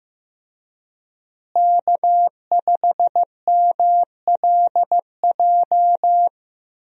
Morse code, heard as a keyed tone, about 15 wpm, 700 Hz. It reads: K5MLJ
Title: USA Call Signs - 5 Characters - Repeated 2x - 15wpm